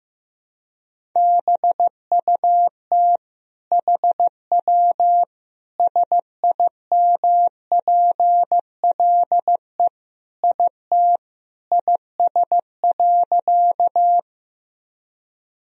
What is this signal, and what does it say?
Morse code, heard as a keyed tone, about 15 wpm, 700 Hz. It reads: BUT HW SIMPLE IT IS.
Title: Sentences from Top 400 Words and Abbreviations 15wpm